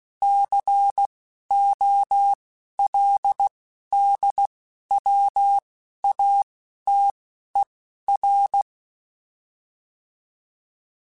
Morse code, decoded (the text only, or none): COLDWATER